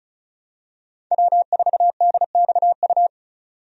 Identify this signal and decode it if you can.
Morse code, heard as a keyed tone, about 36 wpm, 700 Hz. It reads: W4DXU
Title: USA Call Signs - 5 Characters - 35wpm